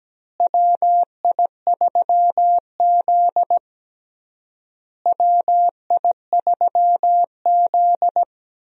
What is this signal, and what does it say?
Morse code, heard as a keyed tone, about 17 wpm, 700 Hz. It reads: WI3Z WI3Z
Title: USA Call Signs - 4 Characters - Repeated 3x - 17wpm